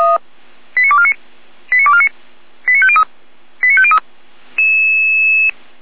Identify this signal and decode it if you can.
Five-tone selective call (ZVEI), heard as quick runs of five tones, tones of about 70 ms. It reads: 89249 89249 89503 89503